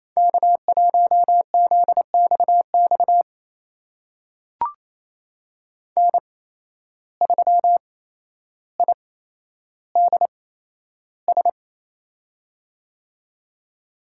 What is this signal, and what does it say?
Morse code, heard as a keyed tone, about 28 wpm, spaced as for 12 wpm, 700 Hz. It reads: K1ZXX N3IDS
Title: USA Call Signs - 5 Characters - 8x Character Spacing - 28wpm